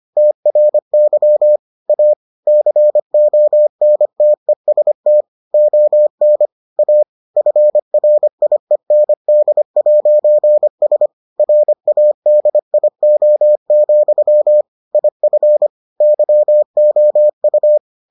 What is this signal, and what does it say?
Morse code, heard as a keyed tone, about 25 wpm, 600 Hz. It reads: TRY A CONTEST ON A FRIEND'S RADIO, IF YOU